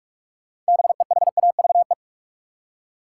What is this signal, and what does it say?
Morse code, heard as a keyed tone, about 45 wpm, 700 Hz. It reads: BEHAVE
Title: Six Letter Words - 45wpm